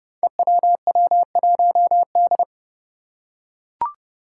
Morse code, heard as a keyed tone, about 32 wpm, 700 Hz. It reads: EWW1D
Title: Five Challenging Letter-Number Combinations 30wpm